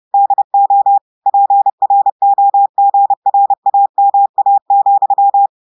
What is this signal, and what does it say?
Morse code, heard as a keyed tone, about 30 wpm, 800 Hz. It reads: DO PROGRAMA,
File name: portugais texte en clair 2 vitesse 30 mots minutes